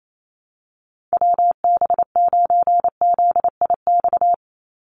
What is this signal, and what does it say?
Morse code, heard as a keyed tone, about 28 wpm, 700 Hz. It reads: WB9ZIX
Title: USA Call Signs 28wpm